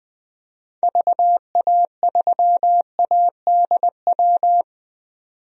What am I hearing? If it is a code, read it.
Morse code, heard as a keyed tone, about 21 wpm, 700 Hz. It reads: VA3ADW
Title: DXCC-100 Call Signs - 6 Characters - 20wpm